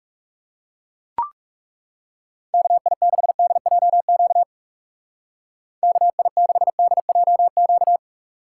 Morse code, heard as a keyed tone, about 39 wpm, 700 Hz. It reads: KI6DJQ KI6DJQ